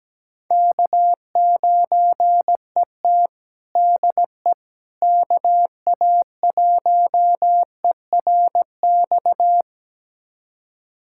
Morse code, heard as a keyed tone, about 17 wpm, 700 Hz. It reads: K9ET DE KA1ERX
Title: USA Callsign DE Callsign 17wpm